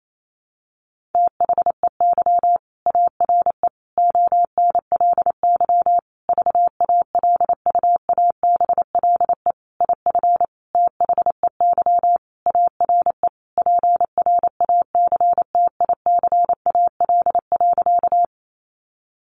Morse code, heard as a keyed tone, about 28 wpm, 700 Hz. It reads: THEY ARE ONLY VALUABLE IF THEY ARE PRACTICAL.
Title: Sentences from Top 1400 Words 28wpm